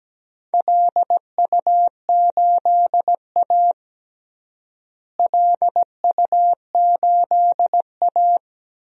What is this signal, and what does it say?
Morse code, heard as a keyed tone, about 17 wpm, 700 Hz. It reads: LU8A LU8A
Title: DXCC-100 Call Signs - 4 Characters - Repeated 3x - 17wpm